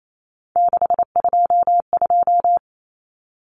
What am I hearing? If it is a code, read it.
Morse code, heard as a keyed tone, about 28 wpm, 700 Hz. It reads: 622
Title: Common Cut Numbers - 3-Digit 28wpm